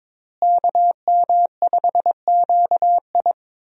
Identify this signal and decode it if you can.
Morse code, heard as a keyed tone, about 22 wpm, 700 Hz. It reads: KM5QI
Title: USA Call Signs - 5 Characters - Repeated 3x - 22wpm